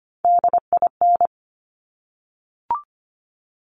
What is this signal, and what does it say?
Morse code, heard as a keyed tone, about 25 wpm, 700 Hz. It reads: DIN
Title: Top Three Letter Combinations in Words 25wpm